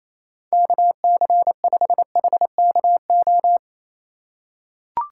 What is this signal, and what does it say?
Morse code, heard as a keyed tone, about 27 wpm, 700 Hz. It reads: KC5HKO
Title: USA Call Signs - 4x Character Spacing - 28wpm